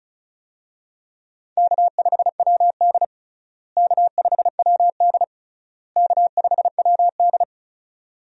Morse code, heard as a keyed tone, about 36 wpm, 700 Hz. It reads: K5WD K5WD K5WD
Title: USA Call Signs - 4 Characters - Repeated 3x - 35wpm